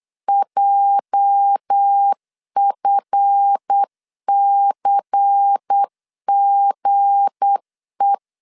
Morse code, heard as a keyed tone, about 8 wpm, 800 Hz. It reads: JFCGE
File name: LEZIONI / 8 YJ45 / 8 YJ45 8wpm 3214